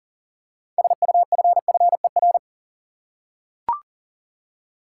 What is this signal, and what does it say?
Morse code, heard as a keyed tone, about 40 wpm, 700 Hz. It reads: SUFFER
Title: Six Letter Words - 40wpm